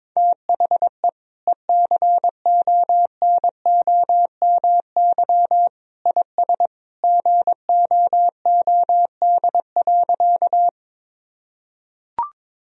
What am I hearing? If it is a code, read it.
Morse code, heard as a keyed tone, about 22 wpm, 700 Hz. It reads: THE ECONOMY IS GOOD.
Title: Sentences from Top 1000 Words - Review Part 3 - 22wpm